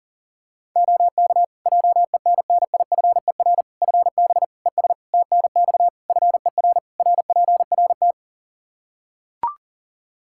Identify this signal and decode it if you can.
Morse code, heard as a keyed tone, about 37 wpm, 700 Hz. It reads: OK JENNIFER FB ES TNX FER RPRT
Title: Practice QSO - Exchange 2 - Info - 40wpm